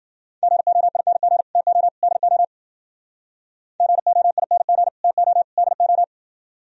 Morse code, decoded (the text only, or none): GOING TO DO GOING TO DO